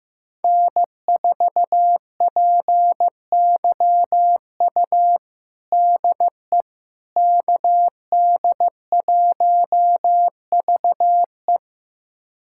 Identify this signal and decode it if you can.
Morse code, heard as a keyed tone, about 15 wpm, 700 Hz. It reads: N4PYU DE KD1VE
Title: USA Callsign DE Callsign - 5 Characters - 15wpm